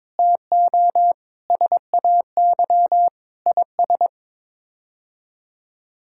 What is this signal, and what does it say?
Morse code, heard as a keyed tone, about 22 wpm, 700 Hz. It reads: TO SAY IS
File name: practice / 22wpm / Sets of 8 Words - 22wpm